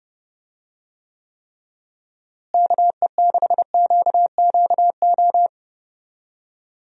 Morse code, heard as a keyed tone, about 30 wpm, 700 Hz. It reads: KE6QQO
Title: USA Call Signs - 6 Characters - Repeated 3x - 30wpm